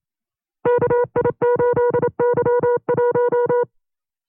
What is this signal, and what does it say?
Morse code, heard as a keyed tone, about 27 wpm, 480 Hz. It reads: KI8Y1
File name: CW-practice/25WPM